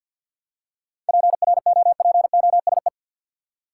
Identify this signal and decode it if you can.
Morse code, heard as a keyed tone, about 51 wpm, 700 Hz. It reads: PROPOSE
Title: Seven Letter Words - 50wpm